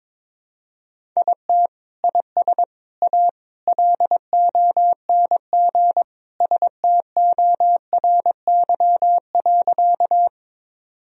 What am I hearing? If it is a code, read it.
Morse code, heard as a keyed tone, about 22 wpm, 700 Hz. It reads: IT IS A LONG STORY.
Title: Sentences from Top 1000 Words - Review Part 1 - 22wpm